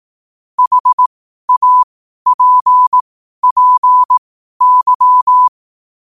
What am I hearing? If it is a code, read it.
Morse code, heard as a keyed tone, about 18 wpm, spaced as for 12 wpm, 1000 Hz. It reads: HAPPY
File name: qotdfiles / 12wpm